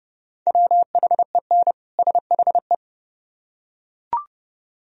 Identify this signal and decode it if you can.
Morse code, heard as a keyed tone, about 30 wpm, 700 Hz. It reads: WHEN SHE